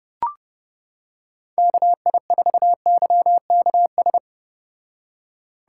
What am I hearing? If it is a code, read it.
Morse code, heard as a keyed tone, about 30 wpm, 700 Hz. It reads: KI4YKS